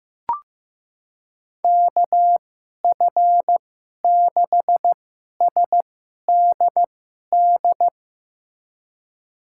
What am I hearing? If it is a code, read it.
Morse code, heard as a keyed tone, about 15 wpm, spaced as for 11 wpm, 700 Hz. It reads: KF6SDD
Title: USA Call Signs - 2x Character Spacing - 15wpm